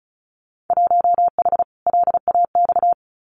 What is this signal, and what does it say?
Morse code, heard as a keyed tone, about 34 wpm, 700 Hz. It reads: 1H LAX
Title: Winter Field Day Exchanges 35wpm